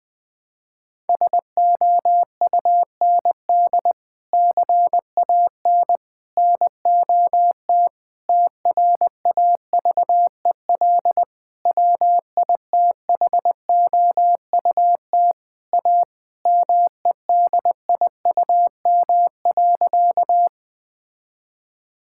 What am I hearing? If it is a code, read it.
Morse code, heard as a keyed tone, about 20 wpm, 700 Hz. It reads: SOUND CAN NOT TRAVEL WITHOUT A MEDIUM.
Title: Sentences from Top 1000 Words - Review Part 6 - 20wpm